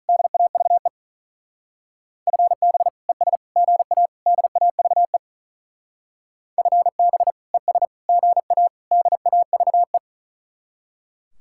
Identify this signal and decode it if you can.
Morse code, heard as a keyed tone, about 45 wpm, 700 Hz. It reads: DAVE FB ES GA DAVE FB ES GA DAVE